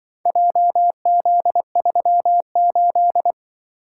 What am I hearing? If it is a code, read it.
Morse code, heard as a keyed tone, about 24 wpm, 700 Hz. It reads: JZ38
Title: Speed-Racing - Four Letter-Number Combinations - 20wpm